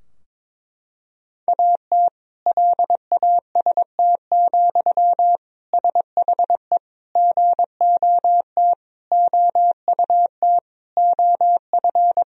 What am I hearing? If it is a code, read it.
Morse code, heard as a keyed tone, about 21 wpm, 700 Hz. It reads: AT LAST, SHE GOT OUT OF
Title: Sentences from Top 1000 Words - Review Part 2 - 22wpm